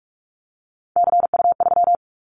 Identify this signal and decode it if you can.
Morse code, heard as a keyed tone, about 46 wpm, 700 Hz. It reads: CU3
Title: Callsign Prefixes for Top 100 DXCC - 45wpm